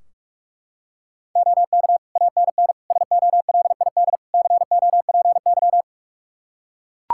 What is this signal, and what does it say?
Morse code, heard as a keyed tone, about 43 wpm, 700 Hz. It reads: OK ANN SOLID COPY
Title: K1USN SST QSO Practice 45wpm